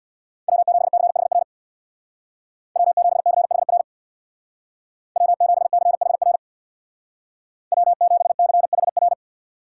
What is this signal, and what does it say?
Morse code, heard as a keyed tone, about 64 wpm, 700 Hz. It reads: W7CHR W7CHR W7CHR W7CHR